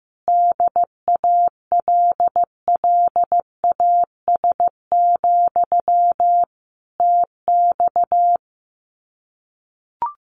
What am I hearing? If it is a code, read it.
Morse code, heard as a keyed tone, about 14 wpm, 700 Hz. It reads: DALLAS, TX
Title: Common QSO Phrases Encore 15wpm